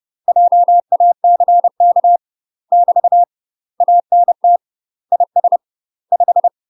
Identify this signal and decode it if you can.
Morse code, heard as a keyed tone, about 30 wpm, 700 Hz. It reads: JACK <BT> ANT IS 5
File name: UK MASTER QSOs TRAINING / 30WPM / UK MASTER QSOs 0002